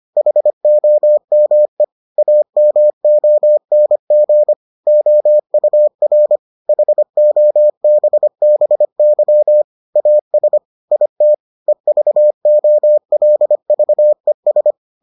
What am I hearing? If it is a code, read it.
Morse code, heard as a keyed tone, about 25 wpm, 600 Hz. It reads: HOME AMONG OUR HOBBY AS IT EVOLVES